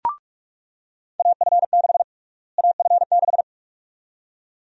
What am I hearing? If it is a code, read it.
Morse code, heard as a keyed tone, about 45 wpm, 700 Hz. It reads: AF6 AF6
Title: USA Call Sign Prefixes - Repeated 2x - 45wpm